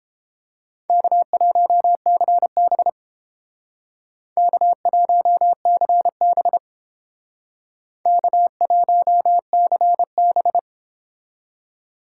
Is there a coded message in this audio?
Yes: Morse code from 0.9 to 10.6 s, heard as a keyed tone, about 31 wpm, 700 Hz. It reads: K1CB K1CB K1CB